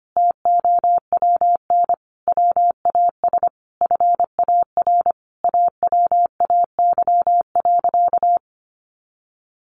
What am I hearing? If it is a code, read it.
Morse code, heard as a keyed tone, about 25 wpm, 700 Hz. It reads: TOWN WAS FAR AWAY.
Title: Sentences from Top 1500 Words 25wpm